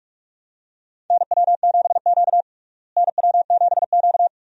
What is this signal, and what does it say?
Morse code, heard as a keyed tone, about 46 wpm, 700 Hz. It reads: NW7Q NW7Q
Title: USA Call Signs - 4 Characters - Repeated 3x - 45wpm